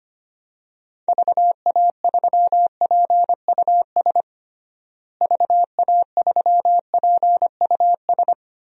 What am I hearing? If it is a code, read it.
Morse code, heard as a keyed tone, about 25 wpm, 700 Hz. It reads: VA3PUS VA3PUS
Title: DXCC-100 Call Signs - 6 Characters - Repeated 2x - 25wpm